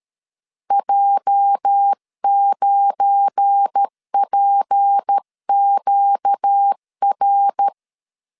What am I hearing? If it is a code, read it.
Morse code, heard as a keyed tone, about 13 wpm, 800 Hz. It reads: J9PQR
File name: LEZIONI / 9 28 12wpm 2351